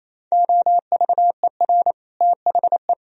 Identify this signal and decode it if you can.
Morse code, heard as a keyed tone, about 34 wpm, 700 Hz. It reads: OVER THE
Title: Speed-Racing - Sets of 4 Words 28wpm